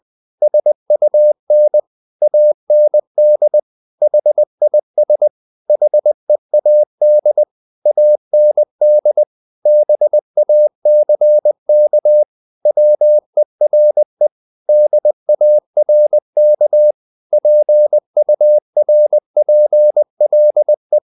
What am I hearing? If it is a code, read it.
Morse code, heard as a keyed tone, about 20 wpm, 600 Hz. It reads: SUN AND HIS HEAD AND BACK WERE DARK PURPLE
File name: the old man and the sea / the old man and the sea part 1 20wpm